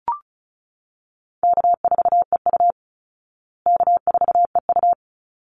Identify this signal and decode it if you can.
Morse code, heard as a keyed tone, about 34 wpm, 700 Hz. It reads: K4EU K4EU
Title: Call Signs Repeated 3x - 35wpm